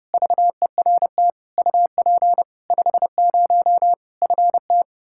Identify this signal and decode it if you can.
Morse code, heard as a keyed tone, about 31 wpm, 700 Hz. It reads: VERT UP 50 FT
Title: Practice QSO - Exchange 2 - Info - 30wpm